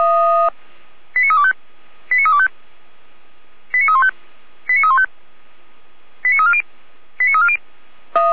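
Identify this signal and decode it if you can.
Five-tone selective call (ZVEI), heard as quick runs of five tones, tones of about 70 ms. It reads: 89325 89325 89315 89315 89340 89340
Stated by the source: Five-tone ZVEI selective call received with an SDR.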